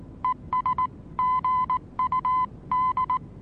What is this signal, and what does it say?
Morse code, heard as a keyed tone, about 19 wpm, 1010 Hz. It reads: ES GUD
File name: repos / Morsekurs / DJ4UF Morsen26